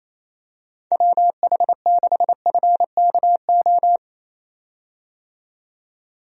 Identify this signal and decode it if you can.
Morse code, heard as a keyed tone, about 28 wpm, 700 Hz. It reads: WH6FKO
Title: USA Call Signs - 6 Characters - 28wpm